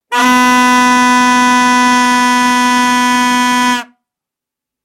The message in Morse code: none